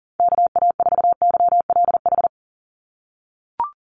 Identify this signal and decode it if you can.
Morse code, heard as a keyed tone, about 40 wpm, 700 Hz. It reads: KA4YLH
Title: USA Call Signs 40wpm